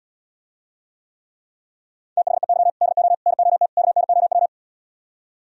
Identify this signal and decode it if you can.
Morse code, heard as a keyed tone, about 77 wpm, 700 Hz. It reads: THEY DO NOT KNOW